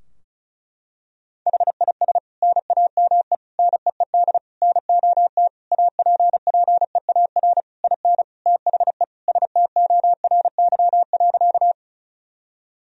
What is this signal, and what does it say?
Morse code, heard as a keyed tone, about 34 wpm, 700 Hz. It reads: HIS NAME NEED NOT APPEAR IN THE STORY.
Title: Sentences from Top 1000 Words - Review Part 2 - 35wpm